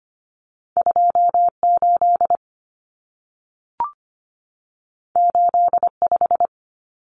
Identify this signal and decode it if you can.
Morse code, heard as a keyed tone, about 25 wpm, 700 Hz. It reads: 28 85
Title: Numbers - 2-Digit 25wpm